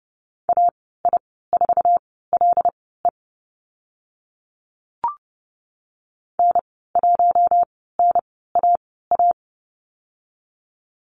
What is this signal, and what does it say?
Morse code, heard as a keyed tone, about 30 wpm, spaced as for 17 wpm, 700 Hz. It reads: AI4LE N1NAA
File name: Lesson 15g - Number 5 - Call Signs 30wpm